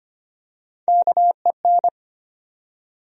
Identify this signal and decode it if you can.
Morse code, heard as a keyed tone, about 25 wpm, 700 Hz. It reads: KEN
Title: USA Common Names - Max 4 Letters - 25wpm